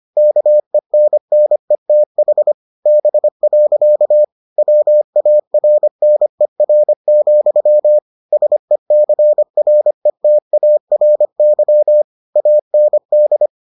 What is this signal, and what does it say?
Morse code, heard as a keyed tone, about 25 wpm, 600 Hz. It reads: KENNETH B. WARNER, SECRETARY AND